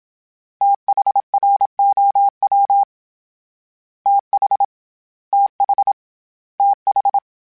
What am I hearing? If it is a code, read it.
Morse code, heard as a keyed tone, about 27 wpm, 800 Hz. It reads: THROW TH TH TH
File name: THs / TH / TH Prefixes 3 @25